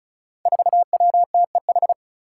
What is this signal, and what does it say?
Morse code, heard as a keyed tone, about 34 wpm, 700 Hz. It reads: 4WTEH
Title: Five Challenging Letter-Number Combinations 35wpm